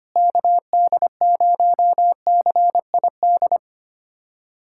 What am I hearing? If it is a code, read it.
Morse code, heard as a keyed tone, about 25 wpm, 700 Hz. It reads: KD0CID
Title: USA Callsign DE Callsign 25wpm